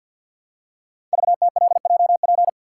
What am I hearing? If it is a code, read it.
Morse code, heard as a keyed tone, about 48 wpm, 700 Hz. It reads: VTLJP
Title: Code Groups - 5 Letters - 50wpm